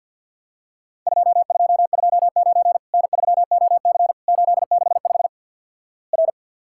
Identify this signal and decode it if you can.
Morse code, heard as a keyed tone, about 50 wpm, 700 Hz. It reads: 2229 N3OC 865 R
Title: Worked All Europe DX Contest - QTC Exchange with Repeats - 50wpm